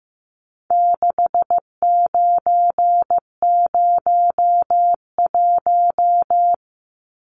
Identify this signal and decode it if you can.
Morse code, heard as a keyed tone, about 15 wpm, 700 Hz. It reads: 6901
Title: Numbers - 4-Digit 15wpm